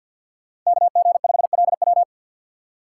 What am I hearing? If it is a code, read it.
Morse code, heard as a keyed tone, about 50 wpm, 700 Hz. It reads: KG5LW
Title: USA Call Signs - 5 Characters - 6x Character Spacing - 50wpm